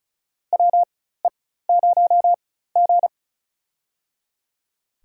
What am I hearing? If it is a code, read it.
Morse code, heard as a keyed tone, about 36 wpm, spaced as for 16 wpm, 700 Hz. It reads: WE0G
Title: USA Call Signs - 4x Character Spacing - 35wpm